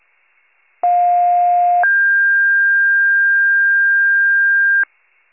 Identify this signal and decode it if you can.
Two-tone page, heard as a one-second tone, then a three-second tone, about 710 then 1600 Hz. It is heard from 0.8 to 4.8 s.